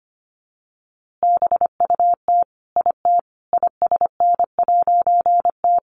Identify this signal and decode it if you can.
Morse code, heard as a keyed tone, about 25 wpm, 700 Hz. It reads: BUT IT ISN'T
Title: A Sound Of Thunder 25wpm